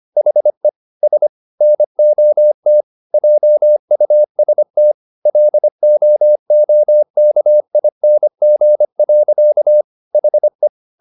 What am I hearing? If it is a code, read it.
Morse code, heard as a keyed tone, about 25 wpm, 600 Hz. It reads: HE S NOT JUST LOOKING. HE
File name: the old man and the sea part 1 25wpm